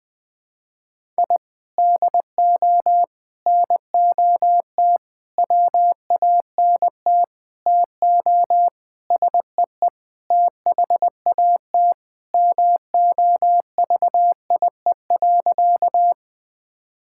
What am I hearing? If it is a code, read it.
Morse code, heard as a keyed tone, about 21 wpm, 700 Hz. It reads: I DO NOT WANT TO SEE THAT MOVIE.
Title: Sentences with 5-Word Phrases - 20wpm